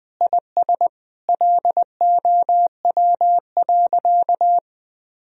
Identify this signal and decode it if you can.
Morse code, heard as a keyed tone, about 20 wpm, 700 Hz. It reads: IS LOW.
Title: Sentences from Top 1000 Words - Review Part 4 - 20wpm